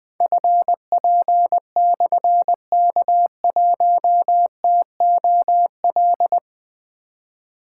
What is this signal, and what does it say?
Morse code, heard as a keyed tone, about 20 wpm, 700 Hz. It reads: FP/K1TOL
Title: Difficult Call Signs 20wpm